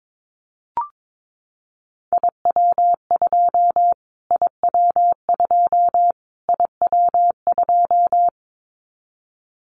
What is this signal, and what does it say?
Morse code, heard as a keyed tone, about 22 wpm, 700 Hz. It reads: IW2 IW2 IW2
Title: Callsign Prefixes for Top 25 DXCC - Repeated 3x - 22wpm